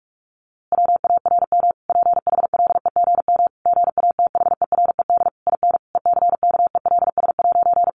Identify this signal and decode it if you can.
Morse code, heard as a keyed tone, about 44 wpm, 700 Hz. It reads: WARM PHLEGM GATHERED IN ECKELS'
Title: A Sound Of Thunder 45wpm